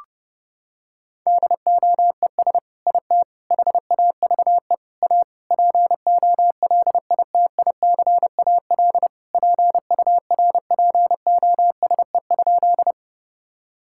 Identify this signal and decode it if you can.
Morse code, heard as a keyed tone, about 30 wpm, 700 Hz. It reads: DOES IT HAVE A POLITICAL PURPOSE?
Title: Sentences from Top 1000 Words - Review Part 5 - 30wpm